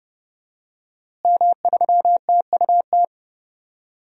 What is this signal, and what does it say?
Morse code, heard as a keyed tone, about 31 wpm, 700 Hz. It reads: M3TUT
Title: DXCC-100 Call Signs - 5 Characters - Repeated 2x - 30wpm